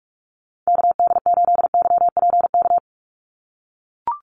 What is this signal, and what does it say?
Morse code, heard as a keyed tone, about 44 wpm, 700 Hz. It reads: KD8YPK
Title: USA Call Signs 45wpm